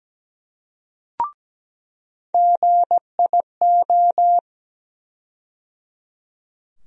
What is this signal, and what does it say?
Morse code, heard as a keyed tone, about 17 wpm, 700 Hz. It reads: GIO
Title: USA Call Sign Postfixes - 3 Letters 17wpm